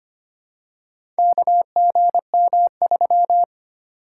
Morse code, heard as a keyed tone, about 25 wpm, 690 Hz. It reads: KGM3
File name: practice / 25wpm / Four Letter-Number Combinations - 25wpm